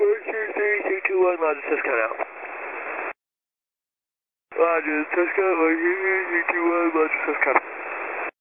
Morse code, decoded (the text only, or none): none